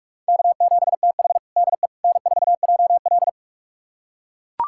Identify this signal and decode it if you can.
Morse code, heard as a keyed tone, about 41 wpm, 700 Hz. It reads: K7TH DE N4JL
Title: USA Callsign DE Callsign - 4 Characters - 45wpm